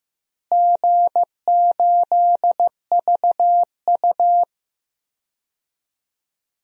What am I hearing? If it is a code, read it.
Morse code, heard as a keyed tone, about 15 wpm, 700 Hz. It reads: G8VU
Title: DXCC-100 Call Signs - 4 Characters - Repeated 2x - 15wpm